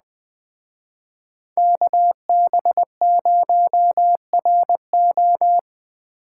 Morse code, heard as a keyed tone, about 20 wpm, 700 Hz. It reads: KB0RO